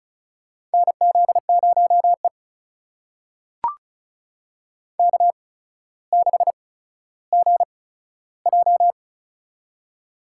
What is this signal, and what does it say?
Morse code, heard as a keyed tone, about 34 wpm, spaced as for 15 wpm, 700 Hz. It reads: NZ0E K6GJ
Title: USA Call Signs - 4 Characters - 8x Character Spacing - 35wpm